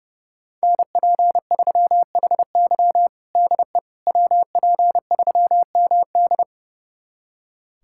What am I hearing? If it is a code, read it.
Morse code, heard as a keyed tone, about 30 wpm, 700 Hz. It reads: NP3HY DE WP3MD
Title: USA Callsign DE Callsign - 5 Characters - 30wpm